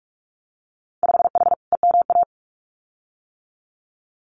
Morse code, heard as a keyed tone, about 46 wpm, 700 Hz. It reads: 5H EMA
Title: Winter Field Day Exchanges Encore 45wpm